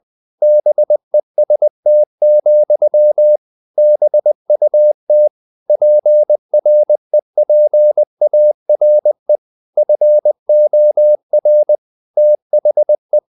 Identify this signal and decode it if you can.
Morse code, heard as a keyed tone, about 20 wpm, 600 Hz. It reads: BEST, BUT PREPARE FOR THE
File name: CW / Practice / EnglishSayings20